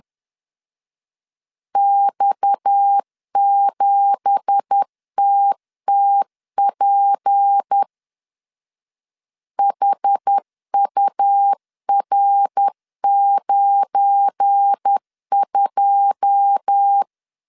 Morse code, heard as a keyed tone, about 10 wpm, 800 Hz. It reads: X7TTP HUR92
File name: CW / DET / 1 9 1500C S B 10wpm 4439